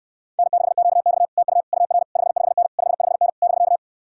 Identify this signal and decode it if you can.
Morse code, heard as a keyed tone, about 72 wpm, 700 Hz. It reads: N6QX TU UR 55N 55N <BK>